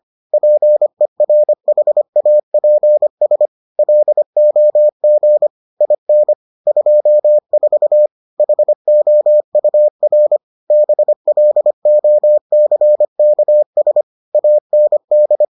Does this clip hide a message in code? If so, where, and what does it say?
Yes: Morse code from 0.3 to 15.4 s, heard as a keyed tone, about 25 wpm, 600 Hz. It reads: PERHAPS LOG IN 24 HOUR BLOCKS AND